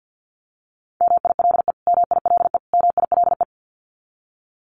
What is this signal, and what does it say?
Morse code, heard as a keyed tone, about 50 wpm, 700 Hz. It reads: MILE MILE MILE
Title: Top 300 Words Repeated 50wpm